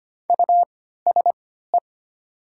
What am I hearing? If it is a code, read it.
Morse code, heard as a keyed tone, about 25 wpm, spaced as for 14 wpm, 700 Hz. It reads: USE
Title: 3x Character Spacing - Top 100 Words - 25wpm